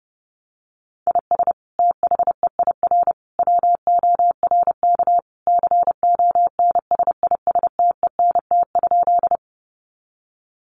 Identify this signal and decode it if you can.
Morse code, heard as a keyed tone, about 30 wpm, 700 Hz. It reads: IS THEIR WORK CONSISTENT?
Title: Sentences from Top 1500 Words 30wpm